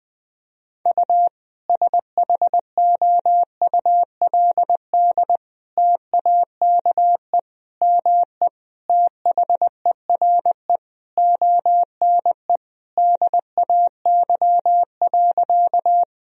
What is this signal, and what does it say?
Morse code, heard as a keyed tone, about 21 wpm, 700 Hz. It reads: U SHOULD TAKE ME THERE ONE DAY.
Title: Sentences from Top 300 Words and Abbreviations 20wpm